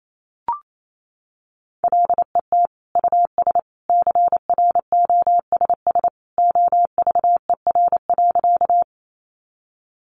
Morse code, heard as a keyed tone, about 28 wpm, 700 Hz. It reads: LET US CROSS OVER.
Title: Sentences from Top 300 Words 28wpm